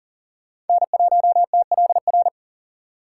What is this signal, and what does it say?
Morse code, heard as a keyed tone, about 51 wpm, 700 Hz. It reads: N1TLR